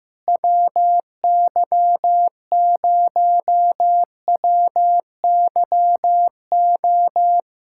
Morse code, heard as a keyed tone, about 15 wpm, 700 Hz. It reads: WY0WYO
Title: SOTA QSO Practice 15wpm